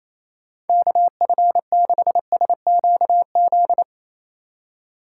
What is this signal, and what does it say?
Morse code, heard as a keyed tone, about 28 wpm, 700 Hz. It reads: KF6SQZ
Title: USA Call Signs - 6 Characters - 2x Character Spacing - 28wpm